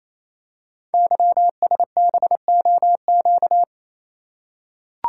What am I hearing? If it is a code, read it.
Morse code, heard as a keyed tone, about 27 wpm, 700 Hz. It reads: YSBOQ
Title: Code Groups - 5 Letters - 28wpm